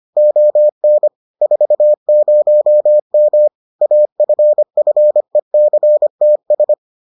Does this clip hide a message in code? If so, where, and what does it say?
Yes: Morse code from 0.2 to 6.7 s, heard as a keyed tone, about 25 wpm, 600 Hz. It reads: ON 40M AFFECTS